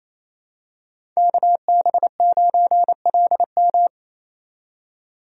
Morse code, heard as a keyed tone, about 28 wpm, 700 Hz. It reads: KB9LM
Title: USA Call Signs - 5 Characters - 28wpm